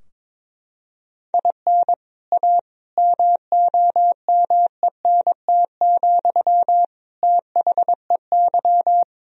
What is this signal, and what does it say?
Morse code, heard as a keyed tone, about 22 wpm, 700 Hz. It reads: IN A MOMENT, THEY
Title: Sentences from Top 1000 Words - Review Part 2 - 22wpm